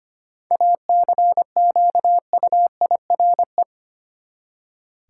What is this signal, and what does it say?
Morse code, heard as a keyed tone, about 25 wpm, 700 Hz. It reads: ACQUIRE
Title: Seven Letter Words - 25wpm